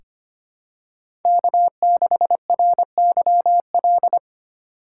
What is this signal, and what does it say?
Morse code, heard as a keyed tone, about 25 wpm, 700 Hz. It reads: K6RYL
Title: USA Call Signs - 5 Characters - 3x Character Spacing - 25wpm